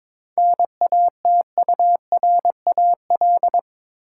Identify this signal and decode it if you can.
Morse code, heard as a keyed tone, about 22 wpm, 700 Hz. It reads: NATURAL
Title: Sentences from Top 700 Words - New Word Drill - 22wpm